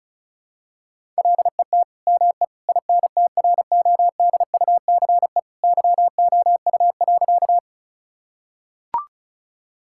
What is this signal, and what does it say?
Morse code, heard as a keyed tone, about 34 wpm, 700 Hz. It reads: LET ME INTRODUCE YOU.
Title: Sentences from Top 1600 Words 35wpm